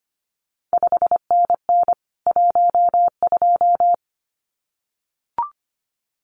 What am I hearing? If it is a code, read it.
Morse code, heard as a keyed tone, about 25 wpm, 700 Hz. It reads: 5NN 12
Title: CQ WW DX Contest Exchange Encore - 25wpm